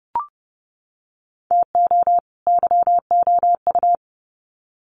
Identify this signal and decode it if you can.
Morse code, heard as a keyed tone, about 30 wpm, 700 Hz. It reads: TO YOU
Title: Sets of 2 Words 30wpm